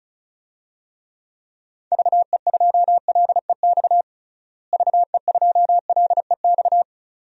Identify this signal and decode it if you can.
Morse code, heard as a keyed tone, about 36 wpm, 700 Hz. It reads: VE2LEX VE2LEX
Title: DXCC-100 Call Signs - 6 Characters - Repeated 2x - 35wpm